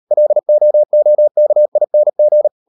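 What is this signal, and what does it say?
Morse code, heard as a keyed tone, about 38 wpm, 600 Hz. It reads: LOOKING
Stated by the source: Morse code practice at 38 WPM, sending Ken Follett's novel Never in English, from CW med Gnister.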